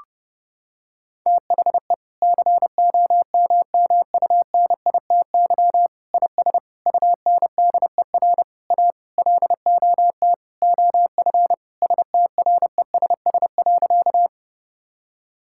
Morse code, heard as a keyed tone, about 28 wpm, 700 Hz. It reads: THE COMMUNITY IS UNDER A LOT OF STRESS.
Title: Sentences from Top 1000 Words - Review Part 6 - 30wpm